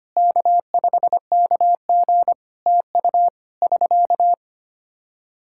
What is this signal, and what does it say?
Morse code, heard as a keyed tone, about 25 wpm, 700 Hz. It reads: K5KG TU <SK>
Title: Practice QSO - Exchange 4 - Ending - 25wpm